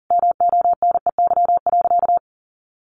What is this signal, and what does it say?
Morse code, heard as a keyed tone, about 40 wpm, 700 Hz. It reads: MONEY.
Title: Sentences from Top 1100 Words 40wpm